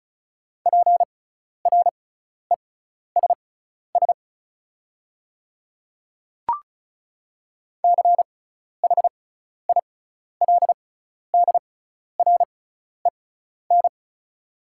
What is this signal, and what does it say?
Morse code, heard as a keyed tone, about 36 wpm, spaced as for 12 wpm, 690 Hz. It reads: PRESS CHILDREN
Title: 6x Character Spacing - Top 300 Words - 35wpm